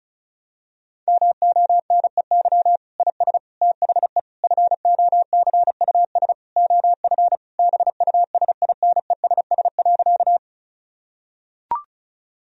Morse code, heard as a keyed tone, about 36 wpm, 700 Hz. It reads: MONEY IS THE FOCUS OF BUSINESS.
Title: Sentences from Top 1000 Words - Review Part 4 - 35wpm